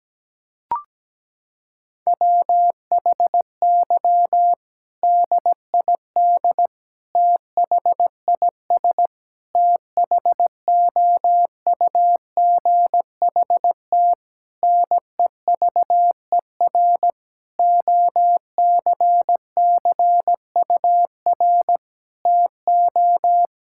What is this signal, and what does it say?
Morse code, heard as a keyed tone, about 17 wpm, 700 Hz. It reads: WHY DID THIS THOUGHT NEVER OCCUR TO
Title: Sentences from Top 1000 Words - Review Part 6 - 17wpm